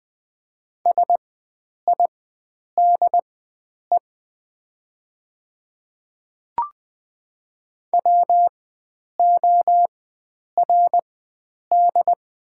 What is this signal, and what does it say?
Morse code, heard as a keyed tone, about 20 wpm, spaced as for 9 wpm, 690 Hz. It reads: SIDE WORD
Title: 4x Character Spacing - Top 100 Words - 20wpm